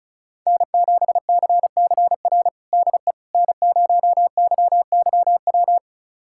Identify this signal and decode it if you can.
Morse code, heard as a keyed tone, about 34 wpm, 700 Hz. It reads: N7CCR DE N0YYW
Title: USA Callsign DE Callsign - 5 Characters - 35wpm